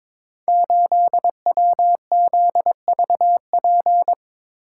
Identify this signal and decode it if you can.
Morse code, heard as a keyed tone, about 22 wpm, 700 Hz. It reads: 8WZVP
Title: Code Groups - 5 Letter:Numbers - 22wpm